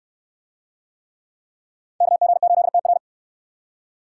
Morse code, heard as a keyed tone, about 60 wpm, 700 Hz. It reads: KG8TR